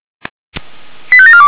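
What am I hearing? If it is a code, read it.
Five-tone selective call (ZVEI), heard as quick runs of five tones, tones of about 70 ms. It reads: 85742
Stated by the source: ZVEI five-tone selective call received by SDR, on the server of Feuerwehr Orlamünde.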